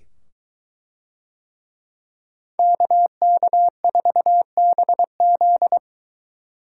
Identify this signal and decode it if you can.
Morse code, heard as a keyed tone, about 21 wpm, 700 Hz. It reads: KK4BZ